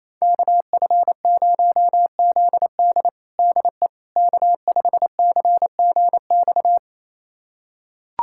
Morse code, heard as a keyed tone, about 28 wpm, 690 Hz. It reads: KF0ZD DE K5CGX
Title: USA Callsign DE Callsign - 5 Characters - 28wpm